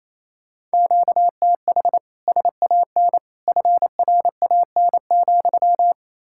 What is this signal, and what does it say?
Morse code, heard as a keyed tone, about 38 wpm, 700 Hz. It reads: QTH SAN FRAN,